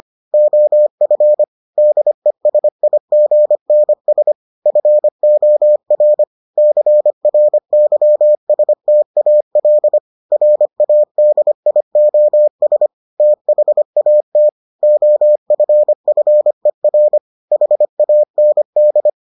Morse code, heard as a keyed tone, about 25 wpm, 600 Hz. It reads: OF DESIGNS FOR CRYSTAL RADIOS THAT OFFER HAND